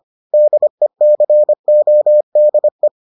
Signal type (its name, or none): Morse code